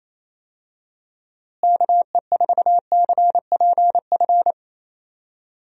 Morse code, heard as a keyed tone, about 28 wpm, 700 Hz. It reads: KE4CPF
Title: USA Call Signs - 6 Characters - Repeated 2x - 28wpm